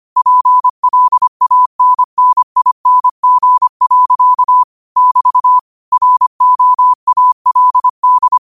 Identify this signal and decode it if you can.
Morse code, heard as a keyed tone, about 25 wpm, 1000 Hz. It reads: PLANNING. <BT> ROALD
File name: qotdfiles / 25wpm